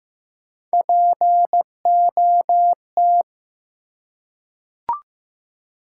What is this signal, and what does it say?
Morse code, heard as a keyed tone, about 15 wpm, 700 Hz. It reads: POT
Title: Three Letter Words - 15wpm